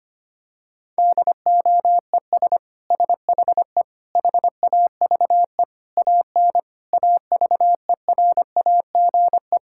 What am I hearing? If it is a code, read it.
Morse code, heard as a keyed tone, about 25 wpm, 700 Hz. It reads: DOES SHE HAVE AN AVERAGE
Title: Sentences from Top 1000 Words - Review Part 6 - 25wpm